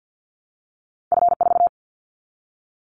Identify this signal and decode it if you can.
Morse code, heard as a keyed tone, about 50 wpm, 700 Hz. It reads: F4
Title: Callsign Prefixes for Top 25 DXCC - 50wpm